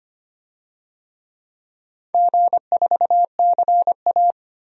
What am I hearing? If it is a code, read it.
Morse code, heard as a keyed tone, about 25 wpm, 700 Hz. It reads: G4CA
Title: DXCC-100 Call Signs - 4 Characters - Repeated 2x - 25wpm